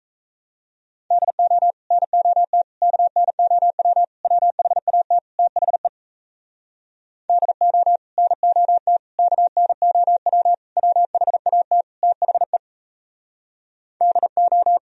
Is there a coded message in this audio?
Yes: Morse code from 1.1 to 14.8 s, heard as a keyed tone, about 37 wpm, 700 Hz. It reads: DO NOT KNOW WHAT THE DO NOT KNOW WHAT THE DO